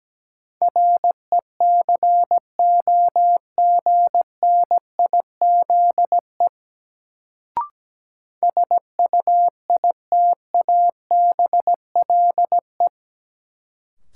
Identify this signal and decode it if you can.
Morse code, heard as a keyed tone, about 17 wpm, 700 Hz. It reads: RECOGNIZE SUITABLE
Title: Sentences from Top 1400 Words - New Word Drill - 17wpm